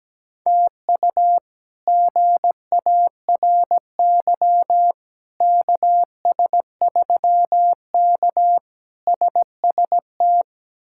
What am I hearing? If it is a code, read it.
Morse code, heard as a keyed tone, about 17 wpm, 700 Hz. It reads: TU GARY KS3K SST
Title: K1USN SST QSO Practice - 17wpm